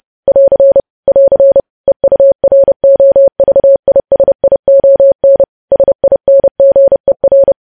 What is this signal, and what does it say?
Morse code, heard as a keyed tone, about 30 wpm, 550 Hz. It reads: <AR> <AR> EUROVISION SINGER
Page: news headlines in morse code at 30 wpm